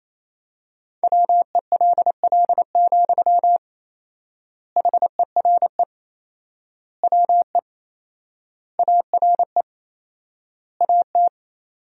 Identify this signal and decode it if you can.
Morse code, heard as a keyed tone, about 28 wpm, 700 Hz. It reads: WELL, HERE WE ARE AT
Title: 4x Word-Spacing - Sentences from Top 300 Words - 28wpm